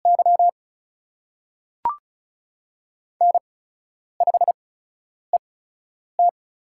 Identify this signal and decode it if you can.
Morse code, heard as a keyed tone, about 36 wpm, spaced as for 10 wpm, 700 Hz. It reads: Y N5ET
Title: USA Call Signs - 4 Characters - 8x Character Spacing - 35wpm